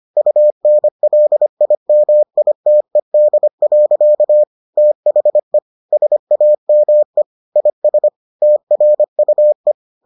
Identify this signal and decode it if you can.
Morse code, heard as a keyed tone, about 25 wpm, 600 Hz. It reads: UNLIMITED. THE SAME IS TRUE